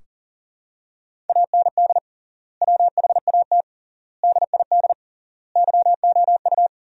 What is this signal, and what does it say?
Morse code, heard as a keyed tone, about 40 wpm, 700 Hz. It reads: AND WHAT DID YOU
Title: 3x Word-Spacing - Sentences from Top 300 Words - 40wpm